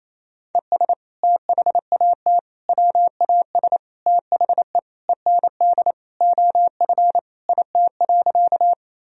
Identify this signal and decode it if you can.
Morse code, heard as a keyed tone, about 28 wpm, 700 Hz. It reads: ES THAT WAS THE END OF IT.
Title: Sentences from Top 200 Words and Abbreviations 28wpm